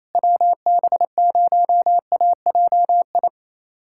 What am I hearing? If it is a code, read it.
Morse code, heard as a keyed tone, about 28 wpm, 700 Hz. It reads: WB0AJI
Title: USA Callsign DE Callsign - 6 Characters - 28wpm